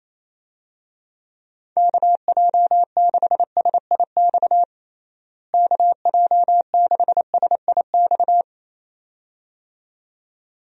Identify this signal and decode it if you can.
Morse code, heard as a keyed tone, about 27 wpm, 700 Hz. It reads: KJ6SIX KJ6SIX
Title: USA Call Signs - 6 Characters - Repeated 2x - 28wpm